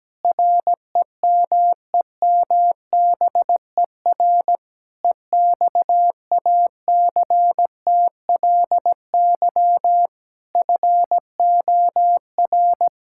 Morse code, heard as a keyed tone, about 17 wpm, 700 Hz. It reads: REMEMBER EXACTLY FOR
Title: Sentences from Top 1000 Words - Review Part 6 - 17wpm